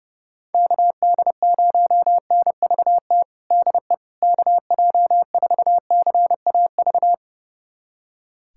Morse code, heard as a keyed tone, about 30 wpm, 700 Hz. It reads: KD0NVT DE KJ4CAV
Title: USA Callsign DE Callsign 30wpm